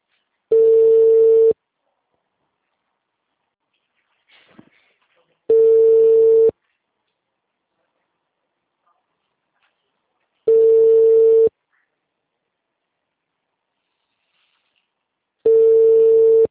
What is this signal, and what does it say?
Ringback tone (China, 450 Hz), with speech.